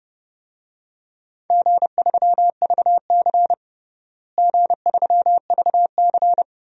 Morse code, heard as a keyed tone, about 31 wpm, 700 Hz. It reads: G3VC G3VC
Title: DXCC-100 Call Signs - 4 Characters - Repeated 2x - 30wpm